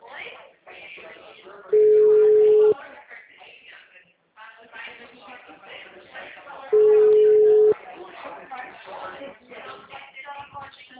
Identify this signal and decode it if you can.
Ringback tone (Europe, 425 Hz), with speech.
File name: out- -2002- - - .83511